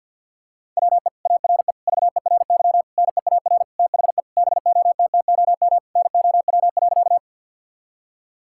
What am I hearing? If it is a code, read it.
Morse code, heard as a keyed tone, about 50 wpm, 700 Hz. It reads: WE ARE VERY NEAR THE BOTTOM NOW.
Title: Sentences from Top 1000 Words - Review Part 6 - 50wpm